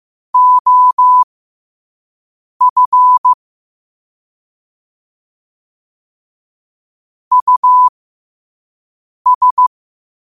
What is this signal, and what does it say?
Morse code, heard as a keyed tone, about 15 wpm, spaced as for 5 wpm, 1000 Hz. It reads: OF US